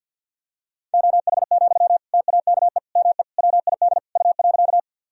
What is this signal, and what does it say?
Morse code, heard as a keyed tone, about 48 wpm, 700 Hz. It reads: OH, TAKE ME WID U.
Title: Sentences from Top 500 Words and Abbreviations 50wpm